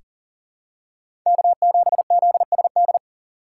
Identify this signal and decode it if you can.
Morse code, heard as a keyed tone, about 40 wpm, 700 Hz. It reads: K7ZSD
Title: USA Call Signs - 5 Characters - 4x Character Spacing - 40wpm